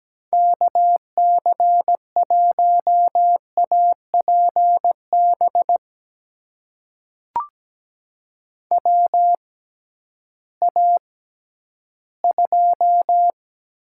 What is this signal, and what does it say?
Morse code, heard as a keyed tone, about 17 wpm, spaced as for 11 wpm, 700 Hz. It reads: KC1APB WA2
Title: USA Call Signs - 6 Characters - 6x Character Spacing - 17wpm